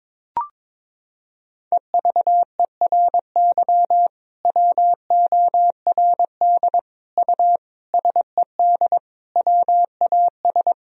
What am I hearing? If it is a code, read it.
Morse code, heard as a keyed tone, about 22 wpm, 700 Hz. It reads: EVERY WORD U SED WAS
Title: Sentences from Top 400 Words and Abbreviations 22wpm